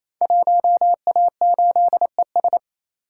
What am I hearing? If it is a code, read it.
Morse code, heard as a keyed tone, about 27 wpm, 690 Hz. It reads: 1A8ES
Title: Code Groups - 5 Letter:Numbers - 28wpm